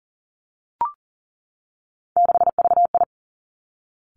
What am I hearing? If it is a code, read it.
Morse code, heard as a keyed tone, about 38 wpm, 700 Hz. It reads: 6VI
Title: Three Challenging Letter-Number Combinations 40wpm